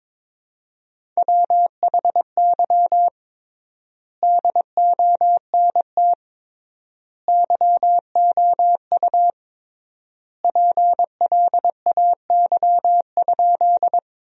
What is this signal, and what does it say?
Morse code, heard as a keyed tone, about 21 wpm, 700 Hz. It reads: WHY DONT YOU PLAY?
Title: 3x Word-Spacing - Sentences from Top 300 Words - 22wpm